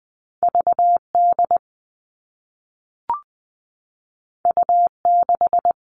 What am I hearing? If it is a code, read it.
Morse code, heard as a keyed tone, about 20 wpm, 700 Hz. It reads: VD U6
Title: All Cut Numbers - 2-Digit 20wpm